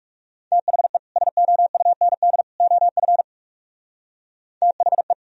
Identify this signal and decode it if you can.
Morse code, heard as a keyed tone, about 38 wpm, 700 Hz. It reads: THE SOUND OF THE